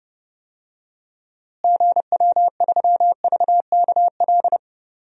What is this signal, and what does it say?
Morse code, heard as a keyed tone, about 30 wpm, 700 Hz. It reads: GW3VKL
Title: DXCC-100 Call Signs - 6 Characters - Repeated 3x - 30wpm